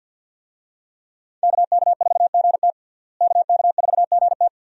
Morse code, heard as a keyed tone, about 46 wpm, 700 Hz. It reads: KK4GT KK4GT
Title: USA Call Signs - 5 Characters - Repeated 2x - 50wpm